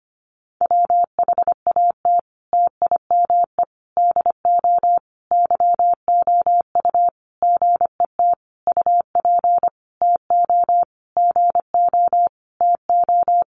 Morse code, heard as a keyed tone, about 25 wpm, 700 Hz. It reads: WHAT TIME DO YOU GET UP TO GO TO
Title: Sentences from Top 1500 Words 25wpm